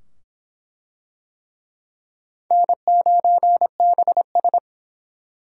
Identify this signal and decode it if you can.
Morse code, heard as a keyed tone, about 24 wpm, 700 Hz. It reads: N9BS